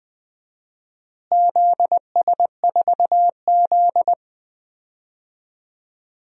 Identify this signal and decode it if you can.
Morse code, heard as a keyed tone, about 20 wpm, 700 Hz. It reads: ZS4Z
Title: DXCC-100 Call Signs - 4 Characters - Repeated 2x - 20wpm